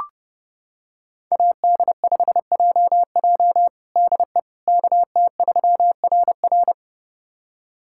Morse code, heard as a keyed tone, about 32 wpm, 700 Hz. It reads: AD5JJ DE KT3RR
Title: USA Callsign DE Callsign - 5 Characters - 30wpm